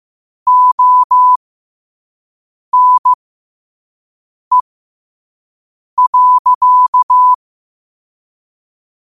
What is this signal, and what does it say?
Morse code, heard as a keyed tone, about 15 wpm, spaced as for 5 wpm, 1000 Hz. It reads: ONE.
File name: 5wpm